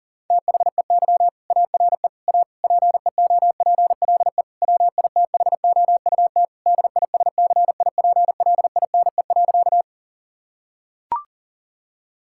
Morse code, heard as a keyed tone, about 40 wpm, 700 Hz. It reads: THEY ARE A PEOPLE WITHOUT DISCIPLINE.
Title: Sentences from Top 1000 Words - Review Part 6 - 40wpm